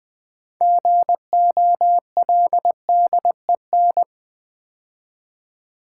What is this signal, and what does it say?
Morse code, heard as a keyed tone, about 20 wpm, 700 Hz. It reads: GOLDEN
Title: Build A Word - 3 Letters At A Time - 6-Letter Words - 20wpm